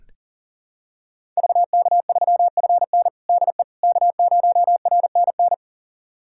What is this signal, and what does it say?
Morse code, heard as a keyed tone, about 40 wpm, 700 Hz. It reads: VK3FN DE K0RNN
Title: USA Callsign DE Callsign - 5 Characters - 40wpm